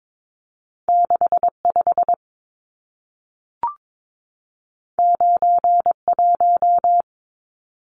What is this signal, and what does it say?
Morse code, heard as a keyed tone, about 22 wpm, 700 Hz. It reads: 65 91
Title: Numbers - 2-Digit 22wpm